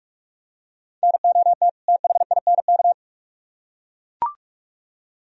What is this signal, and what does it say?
Morse code, heard as a keyed tone, about 44 wpm, 700 Hz. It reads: NOT THINK